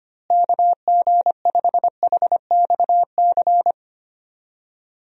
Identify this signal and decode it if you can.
Morse code, heard as a keyed tone, about 25 wpm, 700 Hz. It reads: KG5HXC
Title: USA Call Signs - 6 Characters - Repeated 3x - 25wpm